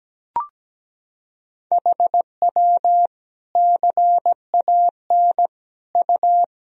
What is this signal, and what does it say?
Morse code, heard as a keyed tone, about 17 wpm, 700 Hz. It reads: HW CAN U
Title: Sentences from Top 200 Words and Abbreviations 17wpm